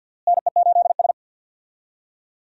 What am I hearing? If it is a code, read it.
Morse code, heard as a keyed tone, about 46 wpm, 700 Hz. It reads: NE8S
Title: USA Call Signs - 4 Characters - Repeated 3x - 50wpm